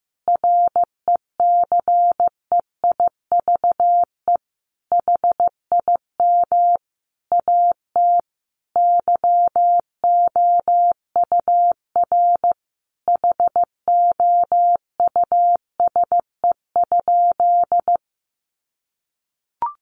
Morse code, heard as a keyed tone, about 15 wpm, 700 Hz. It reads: RECEIVE HIM AT YOUR HOUSE?
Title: Sentences from Top 1000 Words 15wpm